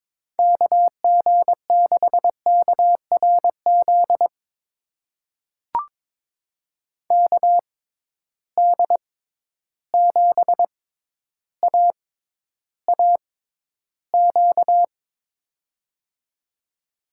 Morse code, heard as a keyed tone, about 23 wpm, spaced as for 11 wpm, 700 Hz. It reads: KG6KRZ KD7AAQ
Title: USA Call Signs - 6 Characters - 6x Character Spacing - 22wpm